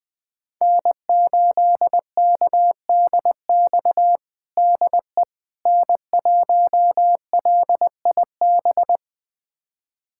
Morse code, heard as a keyed tone, about 20 wpm, 700 Hz. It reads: N8KDX DE N1LIB
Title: USA Callsign DE Callsign - 5 Characters - 20wpm